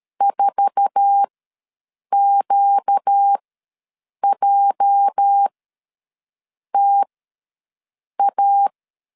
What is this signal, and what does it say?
Morse code, heard as a keyed tone, about 13 wpm, spaced as for 6 wpm, 800 Hz. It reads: 4QJTA
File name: LEZIONI / 8 YJ45 / 8 YJ4 ws 12wpm 3249